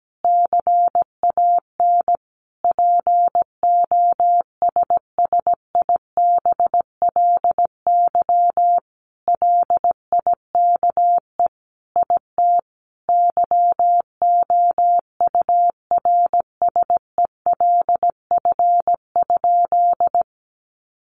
Morse code, heard as a keyed tone, about 17 wpm, 700 Hz. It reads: CAN POSSIBLY LIKE IT YOURSELF?
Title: Sentences from Top 1300 Words 17wpm